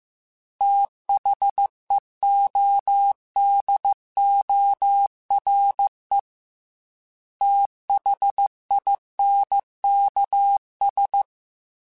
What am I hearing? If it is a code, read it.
Morse code, heard as a keyed tone, about 15 wpm, 800 Hz. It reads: THEODORE THINKS
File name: TH Runs / TH Runs mp3s / TH Runs @13